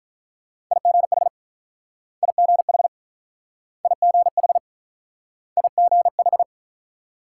Morse code, heard as a keyed tone, about 48 wpm, 700 Hz. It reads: IGH IGH IGH IGH